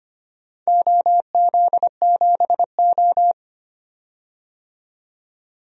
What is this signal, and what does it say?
Morse code, heard as a keyed tone, about 25 wpm, 700 Hz. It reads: OZ7O
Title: DXCC-100 Call Signs - 4 Characters - 25wpm